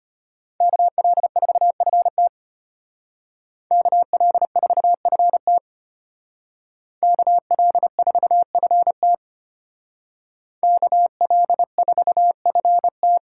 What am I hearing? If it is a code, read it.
Morse code, heard as a keyed tone, about 30 wpm, 700 Hz. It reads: KL4FT KL4FT KL4FT KL4FT
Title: Speed Racing - USA Call Signs 25wpm